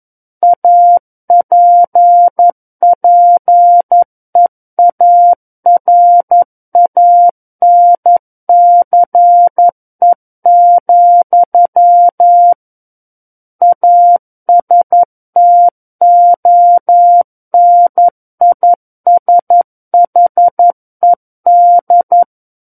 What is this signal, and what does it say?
Morse code, heard as a keyed tone, about 11 wpm, 700 Hz. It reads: APPEARANCE, ASTONISHED
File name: 02-Chapter 2 - 11 WPM